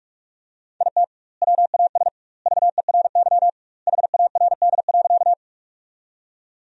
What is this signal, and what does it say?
Morse code, heard as a keyed tone, about 45 wpm, 700 Hz. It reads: IT WAS VERY HARD.
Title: 2x Word-Spacing - Sentences from Top 300 Words - 45wpm